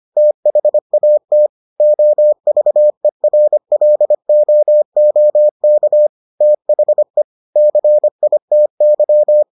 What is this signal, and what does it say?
Morse code, heard as a keyed tone, about 25 wpm, 600 Hz. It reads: THAT OVERLOOK THE CITY